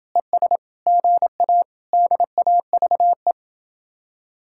Morse code, heard as a keyed tone, about 26 wpm, 700 Hz. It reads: ES GA DAVE